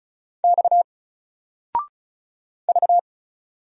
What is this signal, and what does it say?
Morse code, heard as a keyed tone, about 36 wpm, 700 Hz. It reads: X V
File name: Single-Letter-Number-35wpm